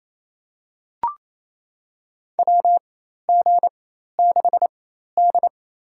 Morse code, heard as a keyed tone, about 28 wpm, spaced as for 13 wpm, 700 Hz. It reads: WG6D
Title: USA Call Signs - 4 Characters - 4x Character Spacing - 28wpm